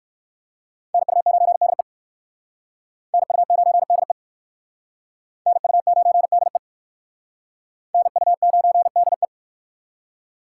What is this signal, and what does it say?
Morse code, heard as a keyed tone, about 60 wpm, 700 Hz. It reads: NU9DE NU9DE NU9DE NU9DE